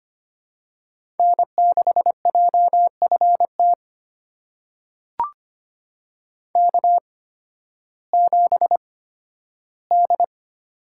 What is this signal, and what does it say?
Morse code, heard as a keyed tone, about 25 wpm, spaced as for 13 wpm, 700 Hz. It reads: N6JFT K7D
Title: USA Call Signs - 5 Characters - 8x Character Spacing - 25wpm